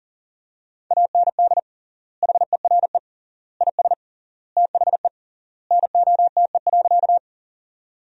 Morse code, heard as a keyed tone, about 37 wpm, 700 Hz. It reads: AND HERE IS THE NOTE.
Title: 3x Word-Spacing - Sentences from Top 500 Words - 40wpm